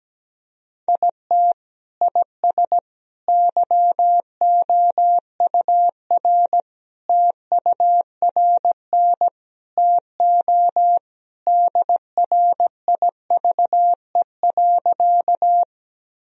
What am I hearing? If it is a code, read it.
Morse code, heard as a keyed tone, about 17 wpm, 700 Hz. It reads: IT IS YOUR TURN TO DRIVE.
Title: Sentences from Top 1000 Words - Review Part 2 - 17wpm